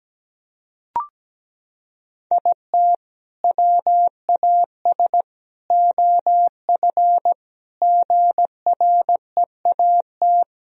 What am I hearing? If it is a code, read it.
Morse code, heard as a keyed tone, about 17 wpm, 700 Hz. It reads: IT WAS OF GREAT
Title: Sentences from Top 1000 Words - Review Part 2 - 17wpm